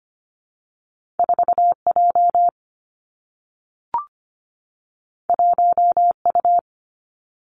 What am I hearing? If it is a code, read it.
Morse code, heard as a keyed tone, about 25 wpm, 700 Hz. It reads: 4 J 1 U
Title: Two Challenging Letter-Number Combinations 25wpm